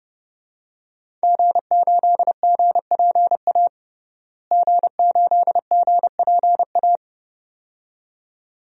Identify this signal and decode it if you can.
Morse code, heard as a keyed tone, about 32 wpm, 700 Hz. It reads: G8GPA G8GPA
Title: DXCC-100 Call Signs - 5 Characters - Repeated 2x - 30wpm